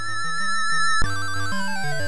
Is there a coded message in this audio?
No.